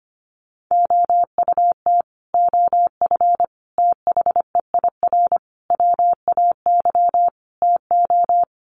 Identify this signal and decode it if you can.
Morse code, heard as a keyed tone, about 25 wpm, 700 Hz. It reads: OUT OF THEIR WAY TO
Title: Sets of 5 Words Encore2 25wpm